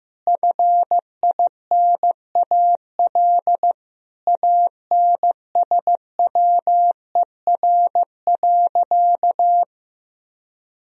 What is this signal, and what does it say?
Morse code, heard as a keyed tone, about 15 wpm, 700 Hz. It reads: FINAL ANSWER.
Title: Sentences from Top 1000 Words - Review Part 2 - 15wpm